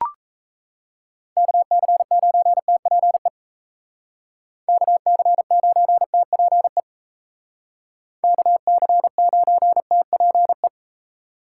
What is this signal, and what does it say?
Morse code, heard as a keyed tone, about 36 wpm, 700 Hz. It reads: KC9TPE KC9TPE KC9TPE